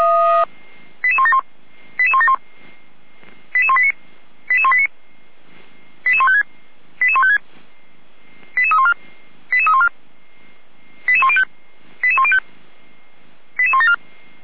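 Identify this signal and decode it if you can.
Five-tone selective call (ZVEI), heard as quick runs of five tones, tones of about 80 ms. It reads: 88171 88171 88189 88189 88156 88156 88314 88314 88105 88105 88174
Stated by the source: ZVEI five-tone selective call 88257 received on an SDR.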